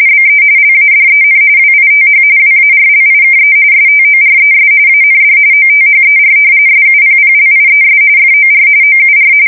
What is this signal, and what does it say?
RTTY, about 45.45 baud. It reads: BE LADEN WITH WARLIKE STORES AND OTHER SUPPLIES FOR OUR
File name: fsk170 / usnavy1